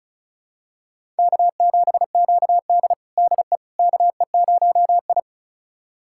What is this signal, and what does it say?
Morse code, heard as a keyed tone, about 36 wpm, 700 Hz. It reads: K7QD DE KE0I
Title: USA Callsign DE Callsign - 4 Characters - 35wpm